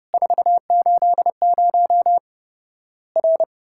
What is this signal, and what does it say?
Morse code, heard as a keyed tone, about 30 wpm, 700 Hz. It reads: 480 R
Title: Worked All Europe DX Contest - QTC Exchange - 30wpm